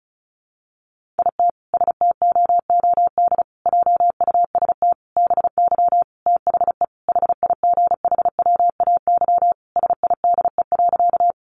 Morse code, heard as a keyed tone, about 34 wpm, 700 Hz. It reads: IT STOOD JUST BY THE HIGHWAY SIDE.
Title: Sentences from Top 1400 Words 35wpm